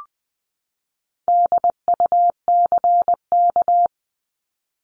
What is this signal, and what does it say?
Morse code, heard as a keyed tone, about 20 wpm, 700 Hz. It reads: DUCK
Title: 260 Lesson 42e - Words 20-20wpm